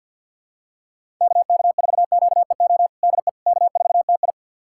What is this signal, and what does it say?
Morse code, heard as a keyed tone, about 51 wpm, 700 Hz. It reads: KK4QEO DE K4TI
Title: USA Callsign DE Callsign 50wpm